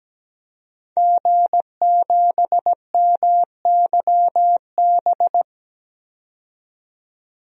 Morse code, heard as a keyed tone, about 17 wpm, 700 Hz. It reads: G7MYB
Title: DXCC-100 Call Signs - 5 Characters - 17wpm